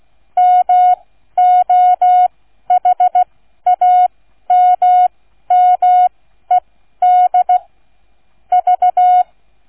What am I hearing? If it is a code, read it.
Morse code, heard as a keyed tone, about 16 wpm, spaced as for 12 wpm, 730 Hz. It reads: MOHAMMED V